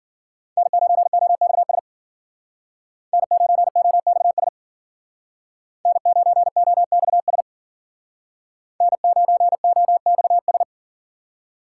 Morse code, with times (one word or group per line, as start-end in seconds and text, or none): 0.6-1.8 N9OXS
3.1-4.5 N9OXS
5.9-7.4 N9OXS
8.8-10.6 N9OXS